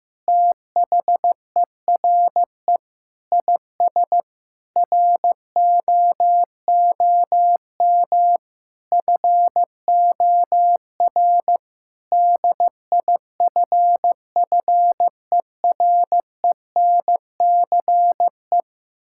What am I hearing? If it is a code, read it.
Morse code, heard as a keyed tone, about 15 wpm, 700 Hz. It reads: THERE IS ROOM FOR DIFFERENCE
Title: Sentences from Top 1600 Words 15wpm